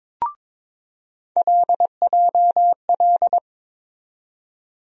Morse code, heard as a keyed tone, about 22 wpm, 700 Hz. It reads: LJL
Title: USA Call Sign Postfixes - 3 Letters 22wpm